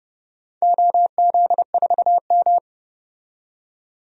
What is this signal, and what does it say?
Morse code, heard as a keyed tone, about 31 wpm, 700 Hz. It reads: OZ4M
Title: DXCC-100 Call Signs - 4 Characters - 30wpm